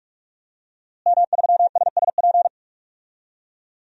Morse code, heard as a keyed tone, about 44 wpm, 700 Hz. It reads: M3SSP
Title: DXCC-100 Call Signs - 5 Characters - Repeated 3x - 45wpm